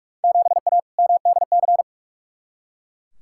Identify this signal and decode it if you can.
Morse code, heard as a keyed tone, about 42 wpm, 700 Hz. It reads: 7A MDC
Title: Field Day Exchanges Encore - 45wpm